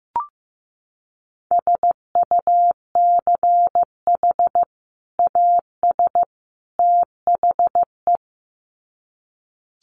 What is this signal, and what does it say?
Morse code, heard as a keyed tone, about 15 wpm, 700 Hz. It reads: SUCH AS THE
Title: Sets of 3 Words 15wpm